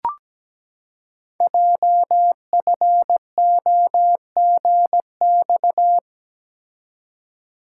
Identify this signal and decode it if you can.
Morse code, heard as a keyed tone, about 17 wpm, 700 Hz. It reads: JFOGX
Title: Code Groups - 5 Letters - 17wpm